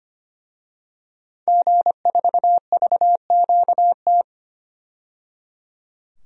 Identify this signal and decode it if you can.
Morse code, heard as a keyed tone, about 25 wpm, 700 Hz. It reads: G4VQT
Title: DXCC-100 Call Signs - 5 Characters - 25wpm